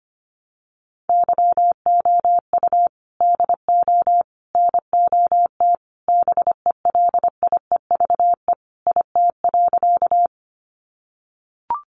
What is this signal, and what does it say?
Morse code, heard as a keyed tone, about 25 wpm, 700 Hz. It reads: YOU DO NOT BELIEVE IT.
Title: Sentences from Top 1000 Words 25wpm